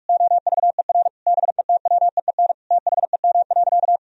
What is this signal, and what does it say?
Morse code, heard as a keyed tone, about 42 wpm, 700 Hz. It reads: OVER BETWEEN THEM.
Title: Sentences from Top 1000 Words - Review Part 1 - 45wpm